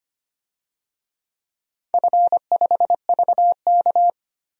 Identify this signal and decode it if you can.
Morse code, heard as a keyed tone, about 25 wpm, 700 Hz. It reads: F5VK
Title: DXCC-100 Call Signs - 4 Characters - 25wpm